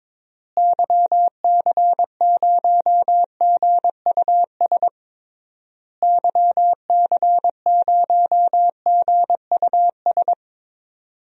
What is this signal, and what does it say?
Morse code, heard as a keyed tone, about 21 wpm, 700 Hz. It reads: YC0GUS YC0GUS
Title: DXCC-100 Call Signs - 6 Characters - Repeated 3x - 22wpm